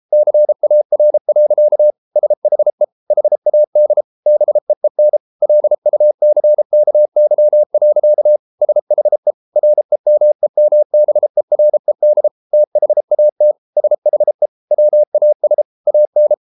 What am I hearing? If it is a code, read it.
Morse code, heard as a keyed tone, about 33 wpm, 600 Hz. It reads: CAR. SHE HAD BEEN LUCKY. SHE REMEMBERED THAT SHE WAS AN